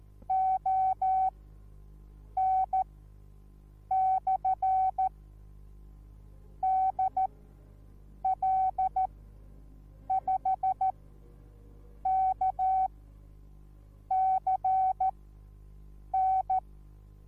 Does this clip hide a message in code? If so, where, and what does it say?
Yes: Morse code from 0.3 to 16.6 s, heard as a keyed tone, about 14 wpm, spaced as for 6 wpm, 760 Hz. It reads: ON/DL5KCN